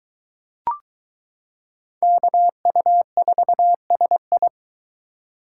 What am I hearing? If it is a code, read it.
Morse code, heard as a keyed tone, about 22 wpm, 700 Hz. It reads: KU4SI